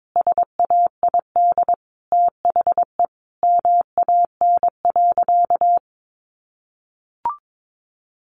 Morse code, heard as a keyed tone, about 22 wpm, 700 Hz. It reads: SAID THE MAN.
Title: Sentences from Top 200 Words 22wpm